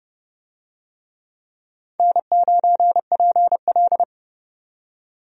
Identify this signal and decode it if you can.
Morse code, heard as a keyed tone, about 30 wpm, 700 Hz. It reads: N9PL
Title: USA Call Signs - 4 Characters - 30wpm